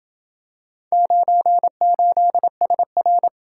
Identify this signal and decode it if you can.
Morse code, heard as a keyed tone, about 28 wpm, 700 Hz. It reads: 98SR